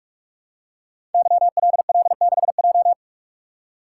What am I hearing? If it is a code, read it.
Morse code, heard as a keyed tone, about 44 wpm, 700 Hz. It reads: YLL6J
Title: Code Groups - 5 Letter:Numbers - 45wpm